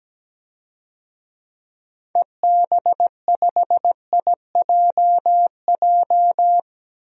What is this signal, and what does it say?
Morse code, heard as a keyed tone, about 17 wpm, 700 Hz. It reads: EB5IJJ
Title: DXCC-100 Call Signs - 6 Characters - Repeated 2x - 17wpm